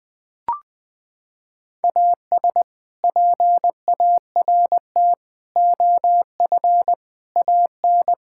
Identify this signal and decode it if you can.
Morse code, heard as a keyed tone, about 20 wpm, 700 Hz. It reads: AS PART OF AN
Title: Sentences with 6-Word Phrases - 20wpm